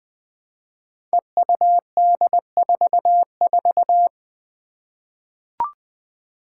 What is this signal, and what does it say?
Morse code, heard as a keyed tone, about 20 wpm, 700 Hz. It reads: EUD44
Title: Five Challenging Letter-Number Combinations 20wpm